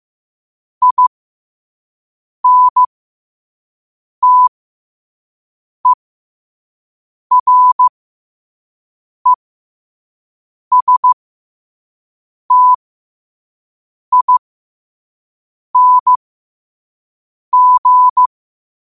Morse code, heard as a keyed tone, about 15 wpm, spaced as for 5 wpm, 1000 Hz. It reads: INTERESTING
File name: qotdfiles / 5wpm